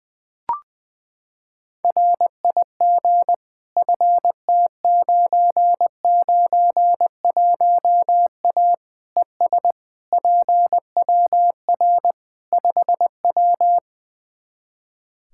Morse code, heard as a keyed tone, about 21 wpm, 700 Hz. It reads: RIG FT991A ES PWR 5W
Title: Practice QSO - Exchange 2 - Info - 20wpm